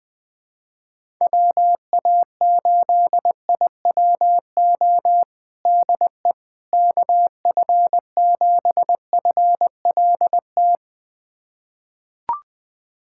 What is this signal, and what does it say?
Morse code, heard as a keyed tone, about 21 wpm, 700 Hz. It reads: WA8IWO DE KF7FLT
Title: USA Callsign DE Callsign 20wpm